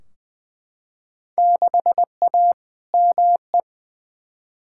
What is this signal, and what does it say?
Morse code, heard as a keyed tone, about 20 wpm, 700 Hz. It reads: 6A ME
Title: Field Day Exchanges - 20wpm